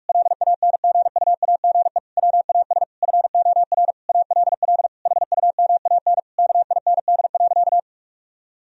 Morse code, heard as a keyed tone, about 41 wpm, 700 Hz. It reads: LANGUAGE WAS FOR ALL HUMAN KIND.
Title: Sentences from Top 1000 Words - Review Part 4 - 45wpm